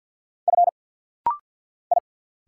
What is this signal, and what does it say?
Morse code, heard as a keyed tone, about 48 wpm, 700 Hz. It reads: F I
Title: Single Letters - Rapid-Fire 50wpm